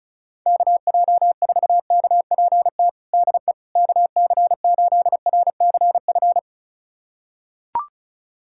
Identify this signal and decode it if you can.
Morse code, heard as a keyed tone, about 36 wpm, 690 Hz. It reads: KJ4KPT DE KC8RCF
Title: USA Callsign DE Callsign - 6 Characters - 35wpm